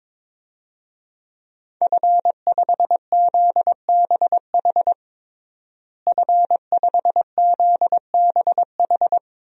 Morse code, heard as a keyed tone, about 22 wpm, 700 Hz. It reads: F5ZBH F5ZBH
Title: DXCC-100 Call Signs - 5 Characters - Repeated 2x - 22wpm